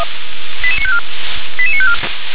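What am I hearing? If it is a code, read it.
Five-tone selective call (ZVEI), heard as quick runs of five tones, tones of about 70 ms. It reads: 88954 88954
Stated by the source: ZVEI five-tone selective call 88920 received by SDR.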